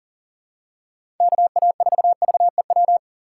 Morse code, heard as a keyed tone, about 40 wpm, 700 Hz. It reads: KA4VEW
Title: USA Call Signs - 6 Characters - 2x Character Spacing - 40wpm